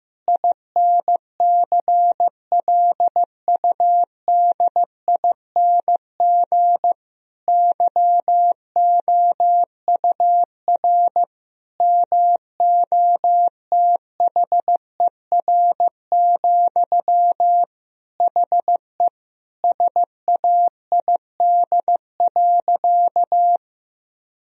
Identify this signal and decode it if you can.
Morse code, heard as a keyed tone, about 15 wpm, 700 Hz. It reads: INCLUDING YOUR MOTHER, HE SAID.
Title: Sentences from Top 1000 Words - Review Part 4 - 15wpm